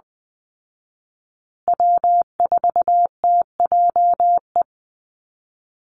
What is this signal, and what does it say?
Morse code, heard as a keyed tone, about 20 wpm, 700 Hz. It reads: W4TJE
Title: Call Signs Repeated 2x - 20wpm